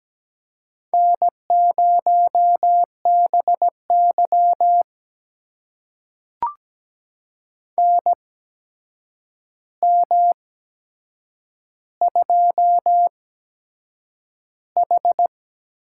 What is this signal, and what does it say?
Morse code, heard as a keyed tone, about 17 wpm, spaced as for 7 wpm, 700 Hz. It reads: N0BY NM2H
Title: USA Call Signs - 4 Characters - 8x Character Spacing - 17wpm